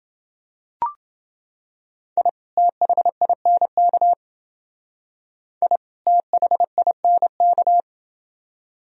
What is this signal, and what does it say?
Morse code, heard as a keyed tone, about 28 wpm, 700 Hz. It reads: I THINK I THINK